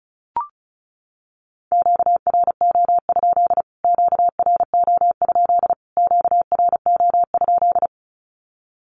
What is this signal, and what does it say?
Morse code, heard as a keyed tone, about 34 wpm, 700 Hz. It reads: QRO? QRO? QRO?
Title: Q-Codes Repeated 3x 35wpm